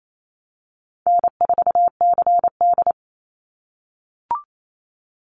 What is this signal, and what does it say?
Morse code, heard as a keyed tone, about 28 wpm, 700 Hz. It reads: N4CD
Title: Easy Call Signs - 28wpm